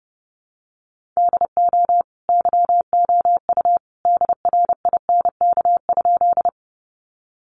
Morse code, heard as a keyed tone, about 30 wpm, 700 Hz. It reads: DO YOU DRINK?
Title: Sentences from Top 1500 Words 30wpm